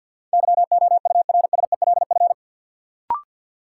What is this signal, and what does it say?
Morse code, heard as a keyed tone, about 53 wpm, 700 Hz. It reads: YOURSELF
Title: Sentences from Top 600 Words - New Word Drill - 50wpm